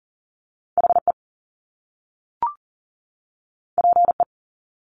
Morse code, heard as a keyed tone, about 38 wpm, 700 Hz. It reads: HE PE
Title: Top Two Letter Combinations in Words 40wpm